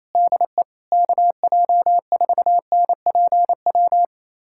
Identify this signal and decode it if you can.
Morse code, heard as a keyed tone, about 28 wpm, 700 Hz. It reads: DE KJ4NPW
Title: USA Callsign DE Callsign - 6 Characters - 28wpm